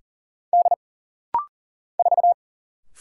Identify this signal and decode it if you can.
Morse code, heard as a keyed tone, about 40 wpm, 700 Hz. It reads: D 4
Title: Single Letter-Number - Rapid-Fire 40wpm